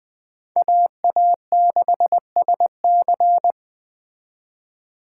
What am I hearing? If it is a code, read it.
Morse code, heard as a keyed tone, about 21 wpm, 700 Hz. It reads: AA6SC
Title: Speed Racing - USA 2x2 Call Signs 17wpm